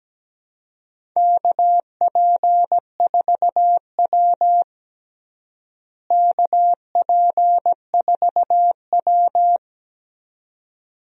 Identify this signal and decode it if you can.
Morse code, heard as a keyed tone, about 17 wpm, 700 Hz. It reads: KP4W KP4W
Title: DXCC-100 Call Signs - 4 Characters - Repeated 2x - 17wpm